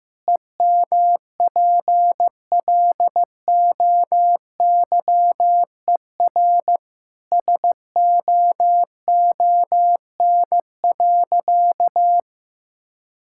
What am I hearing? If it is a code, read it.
Morse code, heard as a keyed tone, about 16 wpm, 700 Hz. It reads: EMPLOYER SOON.
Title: Sentences from Top 1600 Words 15wpm